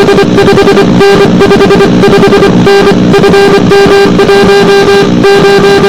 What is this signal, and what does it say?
Two-tone teleprinter signal, about 80 Hz shift.